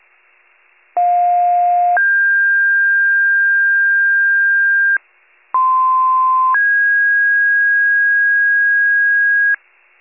Two-tone page, heard as a one-second tone, then a three-second tone, about 710 then 1600 Hz, about 1040 then 1650 Hz. It is heard from 1.0 to 9.6 s.